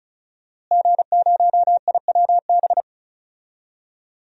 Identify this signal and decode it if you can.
Morse code, heard as a keyed tone, about 36 wpm, 700 Hz. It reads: G0IWB
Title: DXCC-100 Call Signs - 5 Characters - Repeated 2x - 35wpm